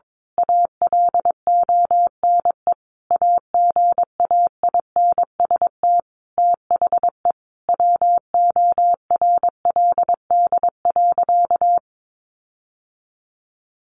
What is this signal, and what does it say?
Morse code, heard as a keyed tone, about 22 wpm, 700 Hz. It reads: ALONE AGAINST THE WORLD.
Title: Sentences from Top 1400 Words 22wpm